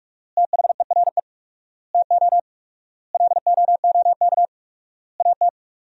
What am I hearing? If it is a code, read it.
Morse code, heard as a keyed tone, about 46 wpm, 700 Hz. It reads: THERE TO LOOK AT
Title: 4x Word-Spacing - Sentences from Top 100 Words - 45wpm